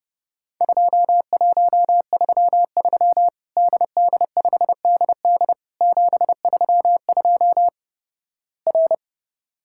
Morse code, heard as a keyed tone, about 32 wpm, 700 Hz. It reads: 2133 DD5DD 732 R
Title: Worked All Europe DX Contest - QTC Exchange - 30wpm